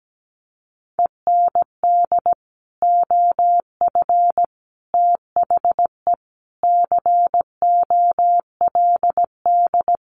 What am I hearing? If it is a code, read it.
Morse code, heard as a keyed tone, about 17 wpm, 700 Hz. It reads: END OF THE COLD
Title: Sets of 5 Words Encore 17wpm